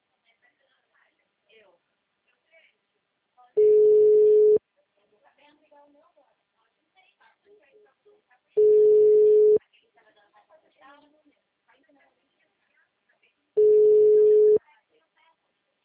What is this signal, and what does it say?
Ringback tone (Europe, 425 Hz), with speech.